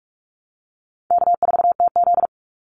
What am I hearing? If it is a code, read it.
Morse code, heard as a keyed tone, about 44 wpm, 700 Hz. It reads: K4TZ
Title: Easy Call Signs Repeated 2x - 45wpm